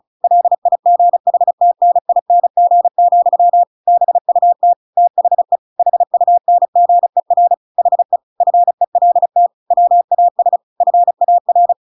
Morse code, heard as a keyed tone, about 35 wpm, 700 Hz. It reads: LIGHTNING, BUT THE HUNGER HE FELT WAS FAR
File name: Pinocchio / Pinocchio ch6 35 35 ik1hge